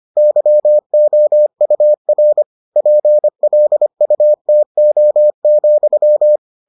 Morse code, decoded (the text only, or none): YOUR PLUTO,